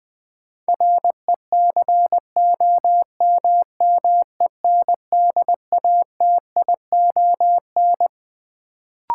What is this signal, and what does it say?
Morse code, heard as a keyed tone, about 20 wpm, 700 Hz. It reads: RECOMMENDATION
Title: Sentences from Top 1400 Words - New Word Drill - 20wpm